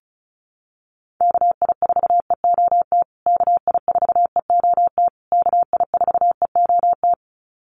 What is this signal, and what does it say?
Morse code, heard as a keyed tone, about 34 wpm, 700 Hz. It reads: KI4EOT KI4EOT KI4EOT
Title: USA Call Signs Repeated 3x 35wpm